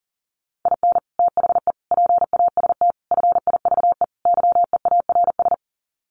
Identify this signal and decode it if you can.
Morse code, heard as a keyed tone, about 40 wpm, 700 Hz. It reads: IN THE PAST FIVE YEARS
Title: Sets of 5 Words Encore 40wpm